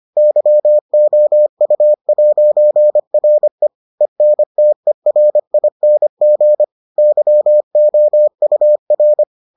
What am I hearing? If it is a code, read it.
Morse code, heard as a keyed tone, about 25 wpm, 600 Hz. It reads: YOU'RE ENTERING YOUR